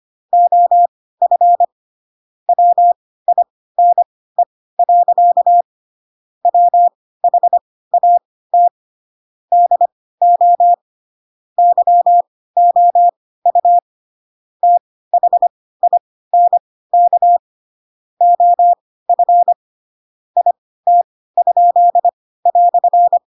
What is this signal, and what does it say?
Morse code, heard as a keyed tone, about 25 wpm, spaced as for 16 wpm, 700 Hz. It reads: OF WINE. WHAT DO YOU THINK OF IT?"
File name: Pinocchio / Pinocchio ch2 10 25 ik1hge